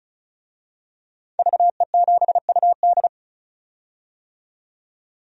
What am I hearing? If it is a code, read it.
Morse code, heard as a keyed tone, about 34 wpm, 700 Hz. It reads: VE7UD
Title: DXCC-100 Call Signs - 5 Characters - 35wpm